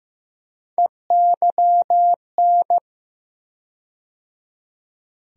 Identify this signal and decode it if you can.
Morse code, heard as a keyed tone, about 15 wpm, 700 Hz. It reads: EYN
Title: USA Call Sign Postfixes - 3 Letters 15wpm